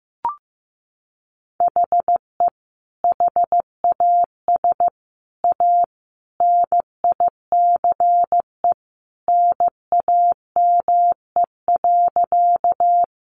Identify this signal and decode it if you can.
Morse code, heard as a keyed tone, about 15 wpm, 700 Hz. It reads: HE HAS A NICE NAME.
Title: Sentences from Top 1100 Words 15wpm